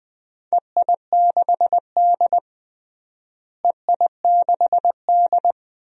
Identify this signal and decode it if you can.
Morse code, heard as a keyed tone, about 20 wpm, 700 Hz. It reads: EI6D EI6D
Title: DXCC-100 Call Signs - 4 Characters - Repeated 2x - 20wpm